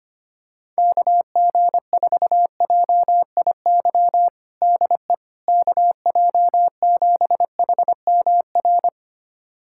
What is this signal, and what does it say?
Morse code, heard as a keyed tone, about 25 wpm, 700 Hz. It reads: KG4JIY DE KJ7HMR
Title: USA Callsign DE Callsign 25wpm